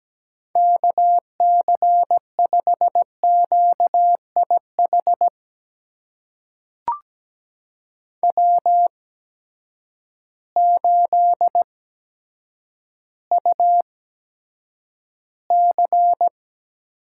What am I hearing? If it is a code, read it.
Morse code, heard as a keyed tone, about 17 wpm, spaced as for 9 wpm, 700 Hz. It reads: KC5QIH W8UC
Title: USA Call Signs - 8x Character Spacing - 17wpm